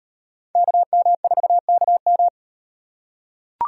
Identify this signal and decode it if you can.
Morse code, heard as a keyed tone, about 38 wpm, 700 Hz. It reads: KM4KM